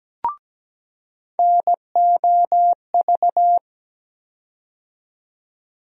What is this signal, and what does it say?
Morse code, heard as a keyed tone, about 17 wpm, 700 Hz. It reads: NOV
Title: USA Call Sign Postfixes - 3 Letters 17wpm